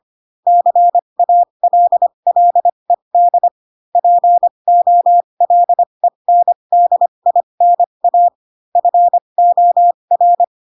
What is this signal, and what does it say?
Morse code, heard as a keyed tone, about 25 wpm, spaced as for 22 wpm, 700 Hz. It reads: CALLED POLENDINA FOR
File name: Pinocchio / Pinocchio ch2 18 25 ik1hge